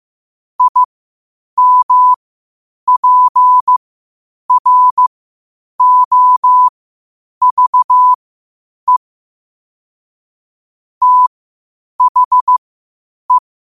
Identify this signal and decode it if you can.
Morse code, heard as a keyed tone, about 15 wpm, spaced as for 8 wpm, 1000 Hz. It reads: IMPROVE THE
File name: qotdfiles / 8wpm